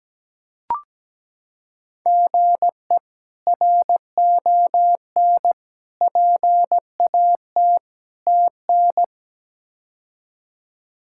Morse code, heard as a keyed tone, about 17 wpm, 700 Hz. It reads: GE RON PAT TN
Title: K1USN SST QSO Practice - 17wpm